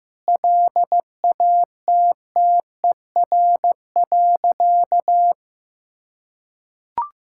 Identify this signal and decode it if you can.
Morse code, heard as a keyed tone, about 15 wpm, 700 Hz. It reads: LATTER.
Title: Sentences from Top 1600 Words 15wpm